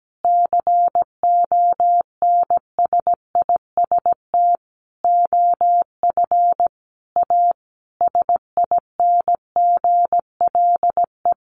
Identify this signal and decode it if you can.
Morse code, heard as a keyed tone, about 17 wpm, 700 Hz. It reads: CONSIST OF A SINGLE
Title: Sentences from Top 1400 Words 17wpm